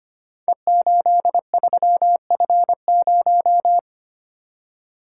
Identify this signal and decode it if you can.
Morse code, heard as a keyed tone, about 25 wpm, 690 Hz. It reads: E83F0
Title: Code Groups - 5 Letter:Numbers - 25wpm